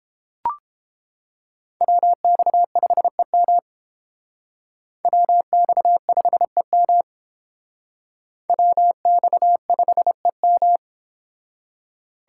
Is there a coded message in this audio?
Yes: Morse code from 1.8 to 10.8 s, heard as a keyed tone, about 31 wpm, 700 Hz. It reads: WX5EM WX5EM WX5EM